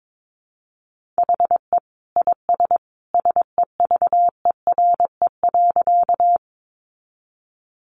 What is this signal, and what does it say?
Morse code, heard as a keyed tone, about 22 wpm, 700 Hz. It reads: HE IS SEVERE.
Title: Sentences from Top 1400 Words 22wpm